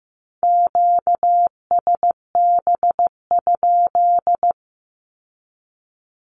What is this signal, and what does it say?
Morse code, heard as a keyed tone, about 15 wpm, 700 Hz. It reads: QSB?
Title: Q-Codes with Spoken Meaning - 15wpm